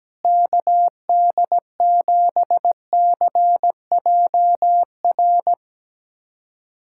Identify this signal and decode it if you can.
Morse code, heard as a keyed tone, about 17 wpm, 700 Hz. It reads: KD7CJR
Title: USA Callsign DE Callsign 17wpm